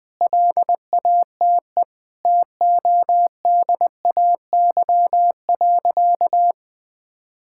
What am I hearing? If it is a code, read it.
Morse code, heard as a keyed tone, about 20 wpm, 700 Hz. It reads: LATE TODAY.
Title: Sentences with 5-Word Phrases - Spoken Beginning - 20wpm